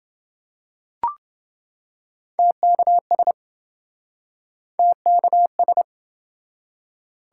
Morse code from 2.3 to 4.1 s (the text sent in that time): TKS